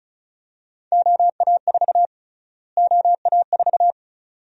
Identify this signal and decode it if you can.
Morse code, heard as a keyed tone, about 36 wpm, 700 Hz. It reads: OA4 OA4
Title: USA Call Sign Prefixes - Repeated 2x - 35wpm